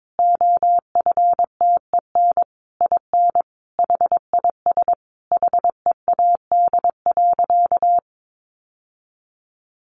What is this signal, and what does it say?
Morse code, heard as a keyed tone, about 22 wpm, 700 Hz. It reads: OFTEN IN HIS HEAD.
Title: Sentences from Top 1400 Words 22wpm